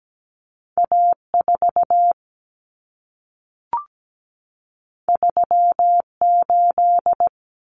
Morse code, heard as a keyed tone, about 17 wpm, 700 Hz. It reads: A4 38
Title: Common Cut Numbers - 2-Digit 17wpm